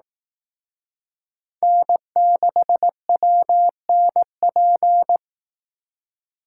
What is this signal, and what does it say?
Morse code, heard as a keyed tone, about 18 wpm, 700 Hz. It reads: N6WNP
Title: Speed Racing - USA 1x3 Call Signs 15wpm